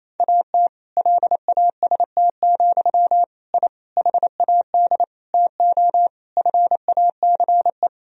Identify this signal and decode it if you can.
Morse code, heard as a keyed tone, about 28 wpm, 700 Hz. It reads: AT LAST, I HAD TO FACE
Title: Sentences from Top 1000 Words - Review Part 2 - 28wpm